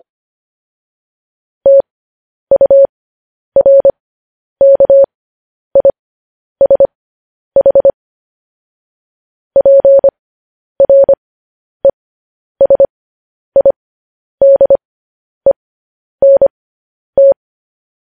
Morse code, heard as a keyed tone, about 25 wpm, spaced as for 10 wpm, 550 Hz. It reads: TURKISH PRESIDENT
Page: news-headlines-in-morse-code-at-10-wpm